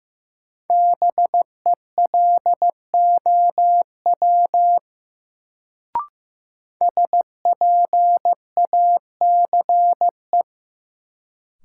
Morse code, heard as a keyed tone, about 15 wpm, 700 Hz. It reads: BELOW SPACE
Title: Sentences from Top 1000 Words - Review Part 5 - New Word Drill - 15wpm